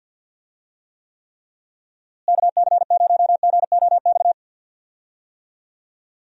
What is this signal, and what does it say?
Morse code, heard as a keyed tone, about 52 wpm, 700 Hz. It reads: KC0GOX
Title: USA Call Signs - 6 Characters - Repeated 3x - 50wpm